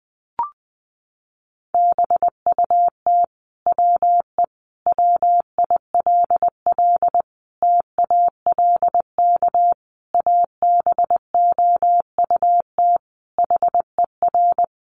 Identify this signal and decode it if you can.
Morse code, heard as a keyed tone, about 20 wpm, 700 Hz. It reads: BUT WE WILL TALK ABOUT HER
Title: Sentences from Top 1200 Words 20wpm